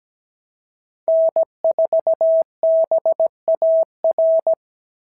Morse code, heard as a keyed tone, about 16 wpm, 650 Hz. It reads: N4BAR
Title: SOTA QSO Practice 17wpm